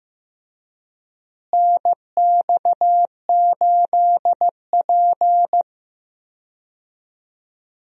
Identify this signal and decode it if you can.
Morse code, heard as a keyed tone, about 15 wpm, 700 Hz. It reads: NX8P
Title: USA Call Signs - 4 Characters - Repeated 2x - 15wpm